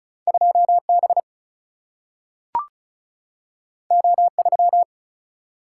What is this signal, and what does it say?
Morse code, heard as a keyed tone, about 36 wpm, 700 Hz. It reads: 2 B O 3
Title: Single-Letter Single-Number 35wpm